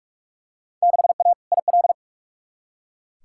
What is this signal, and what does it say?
Morse code, heard as a keyed tone, about 41 wpm, 700 Hz. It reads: 6A IL
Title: Field Day Exchanges - 45wpm